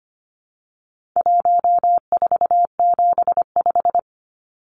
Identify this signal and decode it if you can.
Morse code, heard as a keyed tone, about 25 wpm, 700 Hz. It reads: 1475
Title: Numbers - 4-Digit 25wpm